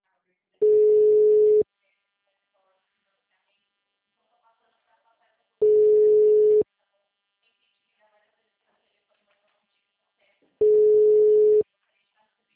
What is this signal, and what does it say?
Ringback tone (Europe, 425 Hz), with speech.
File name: out- -2006- - - .89088